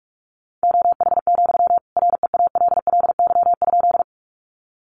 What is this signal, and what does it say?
Morse code, heard as a keyed tone, about 44 wpm, 700 Hz. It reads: OH, REALLY?
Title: Sentences from Top 600 Words 45wpm